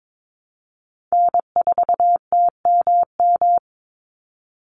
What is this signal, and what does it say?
Morse code, heard as a keyed tone, about 22 wpm, 690 Hz. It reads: N4TMM
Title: Call Signs Repeated 3x - 22wpm